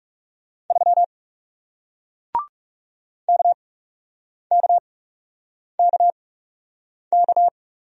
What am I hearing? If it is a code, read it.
Morse code, heard as a keyed tone, about 40 wpm, spaced as for 8 wpm, 700 Hz. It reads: 3 KKKK